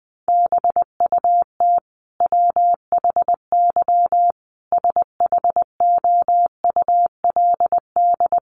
Morse code, heard as a keyed tone, about 20 wpm, 700 Hz. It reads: BUT WHY SHOULD
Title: Sentences from Top 1200 Words 20wpm